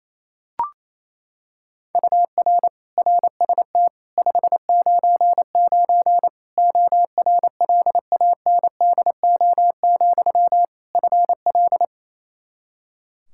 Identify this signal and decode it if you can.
Morse code, heard as a keyed tone, about 28 wpm, 700 Hz. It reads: UR RST 599 ORLANDO, FL
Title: Practice QSO - Exchange 1 - Beginning - 28wpm